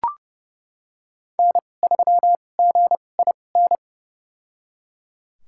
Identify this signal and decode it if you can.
Morse code, heard as a keyed tone, about 32 wpm, spaced as for 22 wpm, 700 Hz. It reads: N3GIN
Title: USA Call Signs - 5 Characters - 2x Character Spacing - 30wpm